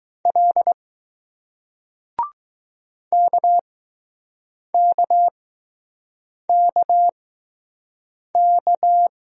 Morse code, heard as a keyed tone, about 20 wpm, spaced as for 6 wpm, 700 Hz. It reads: L KKKK